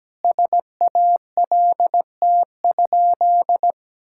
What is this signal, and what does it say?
Morse code, heard as a keyed tone, about 17 wpm, 700 Hz. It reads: SALT?
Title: Sentences from Top 1000 Words - Review Part 5 - 17wpm